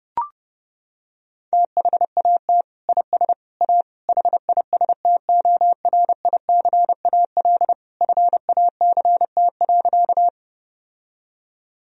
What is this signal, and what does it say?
Morse code, heard as a keyed tone, about 32 wpm, 700 Hz. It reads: THAT IS A HISTORICAL FACT.
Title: Sentences from Top 1000 Words - Review Part 4 - 30wpm